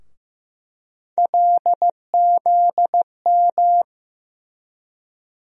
Morse code, heard as a keyed tone, about 15 wpm, 700 Hz. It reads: LZM
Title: USA Call Sign Postfixes - 3 Letters 15wpm